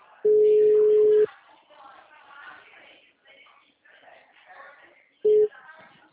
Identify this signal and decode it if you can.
Ringback tone (Europe, 425 Hz), with speech.